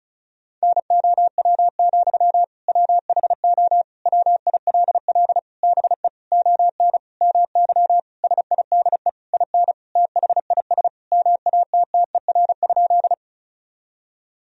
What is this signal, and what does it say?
Morse code, heard as a keyed tone, about 36 wpm, 700 Hz. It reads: NOW, WHO WILL BE ON MY SIDE IN THIS MATTER?
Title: Sentences from Top 1000 Words - Review Part 1 - 35wpm